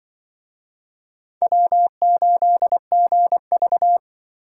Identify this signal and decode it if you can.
Morse code, heard as a keyed tone, about 25 wpm, 700 Hz. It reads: W8GV
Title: Speed Racing - USA Call Signs 20wpm